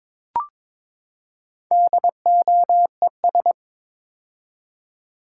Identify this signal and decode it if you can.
Morse code, heard as a keyed tone, about 22 wpm, 700 Hz. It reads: DOES
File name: practice / 22wpm / Build A Phrase - 5-Word Phrases - 22wpm